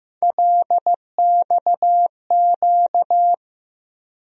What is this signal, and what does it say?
Morse code, heard as a keyed tone, about 15 wpm, 700 Hz. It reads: LXQ
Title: USA Call Sign Postfixes - 3 Letters 15wpm